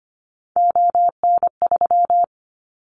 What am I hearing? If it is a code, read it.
Morse code, heard as a keyed tone, about 25 wpm, 700 Hz. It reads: ON3
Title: Callsign Prefixes for Top 25 DXCC - 25wpm